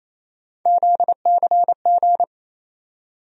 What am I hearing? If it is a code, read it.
Morse code, heard as a keyed tone, about 28 wpm, 700 Hz. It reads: ZCG
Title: USA Call Sign Postfixes - 3 Letters 28wpm